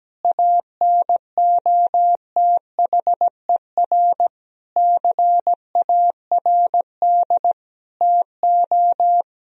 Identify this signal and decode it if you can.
Morse code, heard as a keyed tone, about 17 wpm, 700 Hz. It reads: ANOTHER CARD TO
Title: Sentences from Top 1000 Words - Review Part 4 - 17wpm